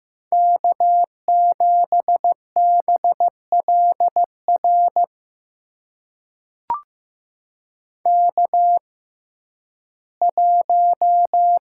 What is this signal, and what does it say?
Morse code, heard as a keyed tone, about 15 wpm, spaced as for 11 wpm, 700 Hz. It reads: K7BLR K1
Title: USA Call Signs - 5 Characters - 6x Character Spacing - 15wpm